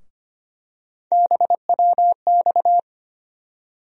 Morse code, heard as a keyed tone, about 25 wpm, 700 Hz. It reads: BWX
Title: USA Call Sign Postfixes - 3 Letters 25wpm